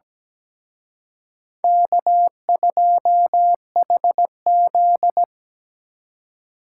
Morse code, heard as a keyed tone, about 20 wpm, 700 Hz. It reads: K2HZ
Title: Speed Racing - USA 1x2 Call Signs 17wpm